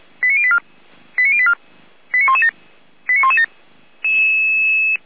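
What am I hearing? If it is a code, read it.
Five-tone selective call (ZVEI), heard as quick runs of five tones, tones of about 70 ms. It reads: 89084 89084 89117 89117
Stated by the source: ZVEI five-tone call 89080 received on SDR.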